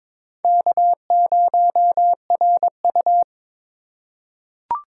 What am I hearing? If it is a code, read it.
Morse code, heard as a keyed tone, about 22 wpm, 700 Hz. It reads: K0RU
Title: USA Call Signs - 4 Characters - 3x Character Spacing - 22wpm